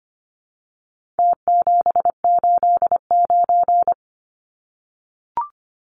Morse code, heard as a keyed tone, about 25 wpm, 700 Hz. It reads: T789
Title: Numbers - Contest Style - 4-Digit 25wpm